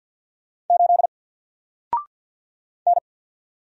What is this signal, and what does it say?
Morse code, heard as a keyed tone, about 50 wpm, 700 Hz. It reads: 8 N
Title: Single Letter-Number Alt 50wpm